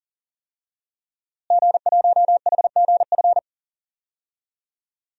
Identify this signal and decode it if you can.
Morse code, heard as a keyed tone, about 40 wpm, 700 Hz. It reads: G1HGF
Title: DXCC-100 Call Signs - 5 Characters - Repeated 2x - 40wpm